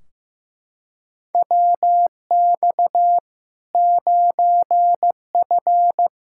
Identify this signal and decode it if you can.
Morse code, heard as a keyed tone, about 16 wpm, 700 Hz. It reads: WX 9F
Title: K1USN SST QSO Practice 15wpm